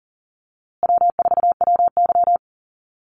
Morse code, heard as a keyed tone, about 40 wpm, 700 Hz. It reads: W4WY
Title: USA Call Signs Repeated 3x 40wpm